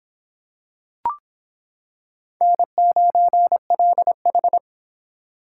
Morse code, heard as a keyed tone, about 26 wpm, 700 Hz. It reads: N9LH